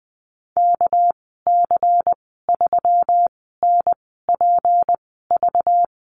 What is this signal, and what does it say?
Morse code, heard as a keyed tone, about 20 wpm, spaced as for 15 wpm, 700 Hz. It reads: KC3NPV
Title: 200 Lesson 32g - Letter k - Call Signs 20wpm